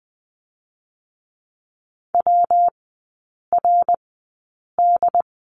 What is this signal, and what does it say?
Morse code, heard as a keyed tone, about 20 wpm, spaced as for 8 wpm, 700 Hz. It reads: WRD
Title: 133 Lesson 22e - Question Mark - Three Characters 20wpm